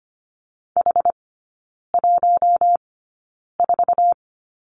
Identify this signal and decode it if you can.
Morse code, heard as a keyed tone, about 25 wpm, spaced as for 9 wpm, 690 Hz. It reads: H14
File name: Lesson 15e - Number 5 - Three Characters 25wpm